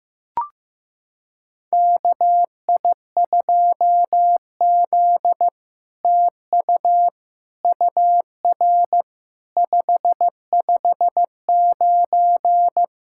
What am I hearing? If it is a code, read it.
Morse code, heard as a keyed tone, about 15 wpm, 700 Hz. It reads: KI2Z TU UR 559
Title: SOTA QSO Practice 15wpm